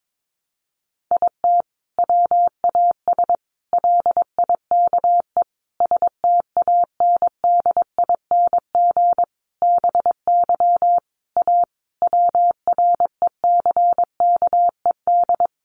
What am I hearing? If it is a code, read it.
Morse code, heard as a keyed tone, about 22 wpm, 700 Hz. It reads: IT WAS LIKE STANDING BY A WRECKED
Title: A Sound Of Thunder 22wpm